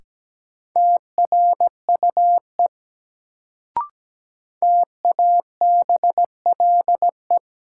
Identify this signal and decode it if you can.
Morse code, heard as a keyed tone, about 17 wpm, 700 Hz. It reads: TRUE TABLE
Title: Sentences from Top 400 Words - New Word Drill - 17wpm